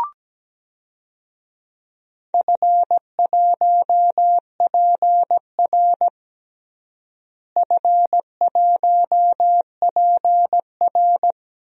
Morse code, heard as a keyed tone, about 17 wpm, 700 Hz. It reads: F1PR F1PR
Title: DXCC-100 Call Signs - 4 Characters - Repeated 3x - 17wpm